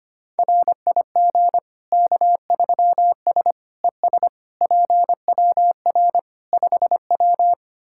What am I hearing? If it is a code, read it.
Morse code, heard as a keyed tone, about 25 wpm, 700 Hz. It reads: RIG K3S ES PWR 5W
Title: K1USN SST QSO Practice 25wpm